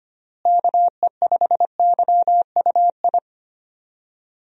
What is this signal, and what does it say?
Morse code, heard as a keyed tone, about 25 wpm, 700 Hz. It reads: KE5YUI
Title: USA Call Signs - 6 Characters - Repeated 2x - 25wpm